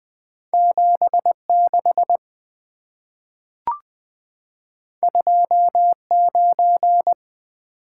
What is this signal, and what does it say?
Morse code, heard as a keyed tone, about 20 wpm, 690 Hz. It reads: 76 29
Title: Two Letter-Number Combinations - 20wpm